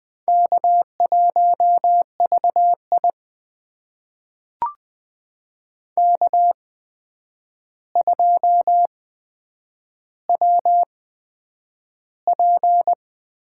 Morse code, heard as a keyed tone, about 20 wpm, spaced as for 9 wpm, 700 Hz. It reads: K1VI K2WP
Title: USA Call Signs - 4 Characters - 8x Character Spacing - 20wpm